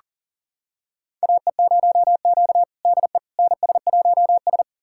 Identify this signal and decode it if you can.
Morse code, heard as a keyed tone, about 40 wpm, 700 Hz. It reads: AE0Q DE NS1S
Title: USA Callsign DE Callsign - 4 Characters - 40wpm